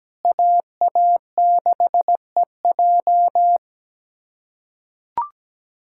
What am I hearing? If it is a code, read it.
Morse code, heard as a keyed tone, about 17 wpm, 700 Hz. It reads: AA6EJ
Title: USA Callsign DE Callsign - 5 Characters - 17wpm